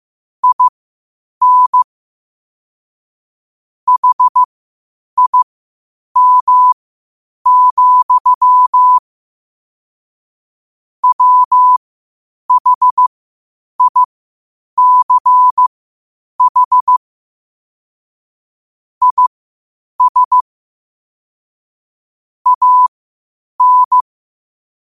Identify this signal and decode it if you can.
Morse code, heard as a keyed tone, about 15 wpm, spaced as for 8 wpm, 1000 Hz. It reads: IN HIM, WHICH IS AN
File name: 8wpm